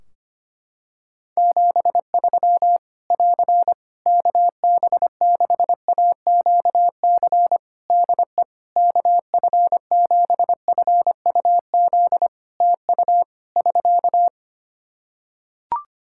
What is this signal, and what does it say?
Morse code, heard as a keyed tone, about 25 wpm, 700 Hz. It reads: 73 <AR> KB6AQC DE KF7FUZ TU <SK>
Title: Practice QSO - Exchange 4 - Ending - 25wpm